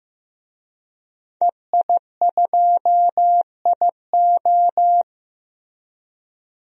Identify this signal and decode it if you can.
Morse code, heard as a keyed tone, about 15 wpm, 700 Hz. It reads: EI2IO
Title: DXCC-100 Call Signs - 5 Characters - 15wpm